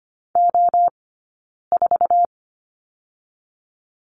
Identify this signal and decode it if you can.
Morse code, heard as a keyed tone, about 25 wpm, 700 Hz. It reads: O 4
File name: Lesson 17d - Letter u - Two Characters 25wpm